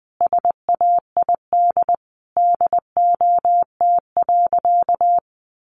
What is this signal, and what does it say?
Morse code, heard as a keyed tone, about 20 wpm, 700 Hz. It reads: SAID DOT.
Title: Sentences from Top 1500 Words 20wpm